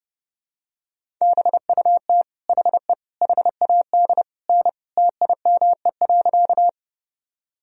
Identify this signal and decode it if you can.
Morse code, heard as a keyed tone, about 30 wpm, 700 Hz. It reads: BUT HE HAD N TIME.
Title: Sentences from Top 100 Words and Abbreviations 30wpm